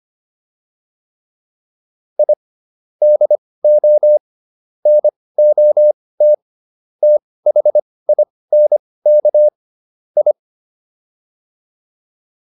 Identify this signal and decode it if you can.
Morse code, heard as a keyed tone, about 25 wpm, spaced as for 18 wpm, 600 Hz. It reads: I DO NOT THINK I